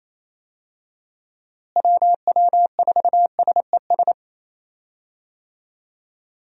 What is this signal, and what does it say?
Morse code, heard as a keyed tone, about 28 wpm, 700 Hz. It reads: WW4SES
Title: USA Call Signs - 6 Characters - Repeated 2x - 28wpm